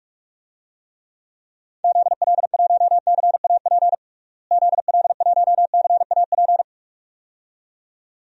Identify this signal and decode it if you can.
Morse code, heard as a keyed tone, about 46 wpm, 700 Hz. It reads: ZL1CAP ZL1CAP
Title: DXCC-100 Call Signs - 6 Characters - Repeated 2x - 45wpm